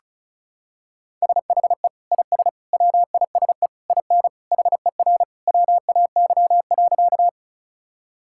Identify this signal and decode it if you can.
Morse code, heard as a keyed tone, about 34 wpm, 700 Hz. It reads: SHE IS WISE IN HER WAY.
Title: Sentences from Top 1600 Words 35wpm